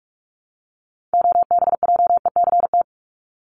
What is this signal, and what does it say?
Morse code, heard as a keyed tone, about 44 wpm, 700 Hz. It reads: OBJECT
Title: Top 500 Words 45wpm